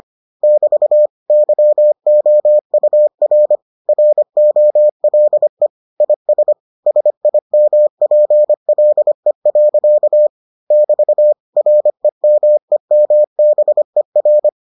Morse code, heard as a keyed tone, about 25 wpm, 600 Hz. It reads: <BT> YOUR ROLE IS SIMPLE. <BT> REMEMBER